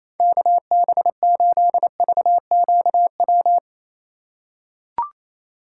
Morse code, heard as a keyed tone, about 28 wpm, 700 Hz. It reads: KB8VQW
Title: USA Call Signs - 4x Character Spacing - 28wpm